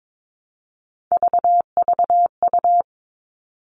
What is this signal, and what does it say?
Morse code, heard as a keyed tone, about 22 wpm, 700 Hz. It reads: VVU
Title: All Cut Numbers - 3-Digit 22wpm